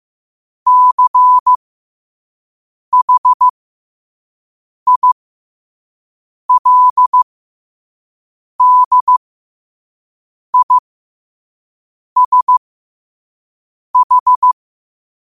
Morse code, heard as a keyed tone, about 15 wpm, spaced as for 5 wpm, 1000 Hz. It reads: CHILDISH
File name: qotdfiles / 5wpm